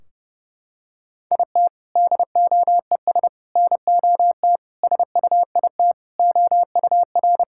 Morse code, heard as a keyed tone, about 30 wpm, 700 Hz. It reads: IT DOES NOT SUIT OUR
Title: Sentences from Top 1600 Words 30wpm